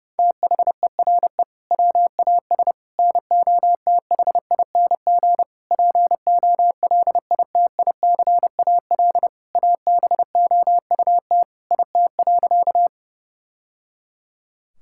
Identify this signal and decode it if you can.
Morse code, heard as a keyed tone, about 30 wpm, 700 Hz. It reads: THERE WAS NOTHING POLITICAL ABOUT IT.
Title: Sentences from Top 1000 Words - Review Part 5 - 30wpm